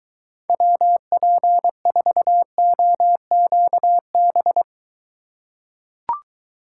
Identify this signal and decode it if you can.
Morse code, heard as a keyed tone, about 23 wpm, 700 Hz. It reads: WP4OQB